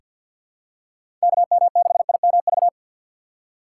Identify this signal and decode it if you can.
Morse code, heard as a keyed tone, about 48 wpm, 700 Hz. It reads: KM6IMV
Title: USA Call Signs - 6 Characters - 6x Character Spacing - 50wpm